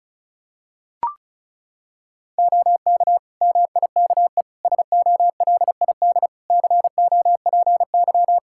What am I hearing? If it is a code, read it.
Morse code, heard as a keyed tone, about 36 wpm, 700 Hz. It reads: OK MIKE SOLID COPY
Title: Practice QSO - Exchange 3 - More Info - 35wpm